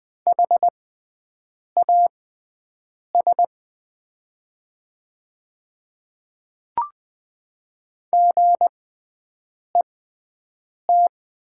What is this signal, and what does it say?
Morse code, heard as a keyed tone, about 20 wpm, spaced as for 7 wpm, 700 Hz. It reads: HAS GET
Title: 6x Character Spacing - Top 200 Words - 20wpm